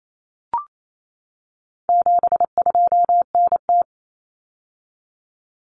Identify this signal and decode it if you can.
Morse code, heard as a keyed tone, about 28 wpm, 700 Hz. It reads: 72NT
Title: Common Cut Numbers - 4-Digit 28wpm